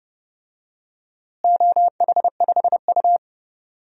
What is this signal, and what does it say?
Morse code, heard as a keyed tone, about 30 wpm, 700 Hz. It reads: OH5U
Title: DXCC-100 Call Signs - 4 Characters - Repeated 3x - 30wpm